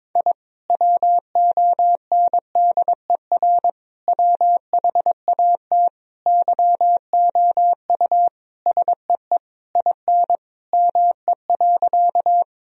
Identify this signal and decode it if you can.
Morse code, heard as a keyed tone, about 22 wpm, 700 Hz. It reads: I WONDER WHAT YOU SEE IN ME.
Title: Sentences from Top 1000 Words - Review Part 3 - 22wpm